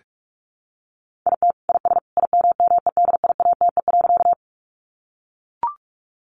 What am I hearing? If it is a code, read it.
Morse code, heard as a keyed tone, about 44 wpm, 700 Hz. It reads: IT IS IMMEDIATE.
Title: Sentences from Top 1200 Words 45wpm